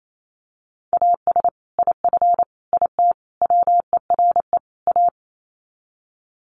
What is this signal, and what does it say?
Morse code, heard as a keyed tone, about 28 wpm, 700 Hz. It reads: AS IF IT WERE A
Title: Sets of 5 Words 28wpm